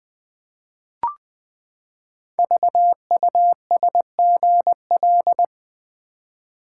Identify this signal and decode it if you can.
Morse code, heard as a keyed tone, about 20 wpm, 700 Hz. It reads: VUSGL
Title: Code Groups - 5 Letters - 20wpm